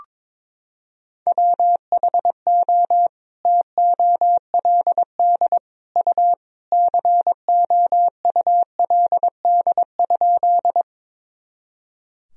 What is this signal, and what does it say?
Morse code, heard as a keyed tone, about 21 wpm, 700 Hz. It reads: WHO TOLD U COULD?
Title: Sentences from Top 400 Words and Abbreviations 22wpm